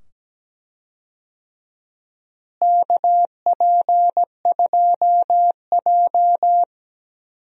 Morse code, heard as a keyed tone, about 17 wpm, 700 Hz. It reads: KP2J
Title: DXCC-100 Call Signs - 4 Characters - Repeated 2x - 17wpm